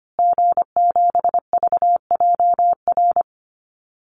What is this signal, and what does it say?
Morse code, heard as a keyed tone, about 25 wpm, 700 Hz. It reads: G7VJR
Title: Call Signs Repeated 3x - 25wpm